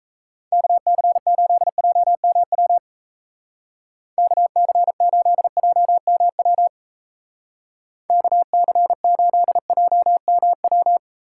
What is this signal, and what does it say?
Morse code, heard as a keyed tone, about 33 wpm, 700 Hz. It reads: KC8JMW KC8JMW KC8JMW
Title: Speed Racing - USA Call Signs 28wpm